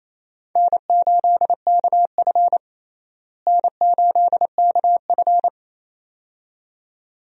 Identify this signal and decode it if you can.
Morse code, heard as a keyed tone, about 28 wpm, 700 Hz. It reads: N8KF N8KF
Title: USA Call Signs - 4 Characters - Repeated 2x - 28wpm